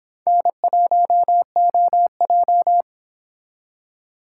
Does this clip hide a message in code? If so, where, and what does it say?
Yes: Morse code from 0.3 to 2.8 s, heard as a keyed tone, about 26 wpm, 700 Hz. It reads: N1OJ